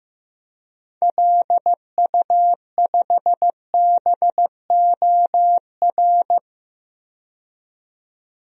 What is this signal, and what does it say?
Morse code, heard as a keyed tone, about 16 wpm, 700 Hz. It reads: LU5BOR
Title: DXCC-100 Call Signs - 6 Characters - 15wpm